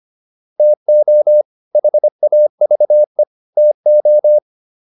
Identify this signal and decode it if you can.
Morse code, heard as a keyed tone, about 25 wpm, 600 Hz. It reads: TO HAVE TO
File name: WD205 25